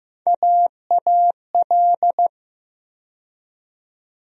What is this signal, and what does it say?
Morse code, heard as a keyed tone, about 15 wpm, 700 Hz. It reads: AAL
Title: USA Call Sign Postfixes - 3 Letters 15wpm